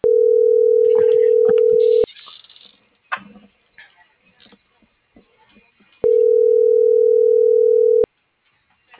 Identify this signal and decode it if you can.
Ringback tone (North America), 440 + 480 Hz.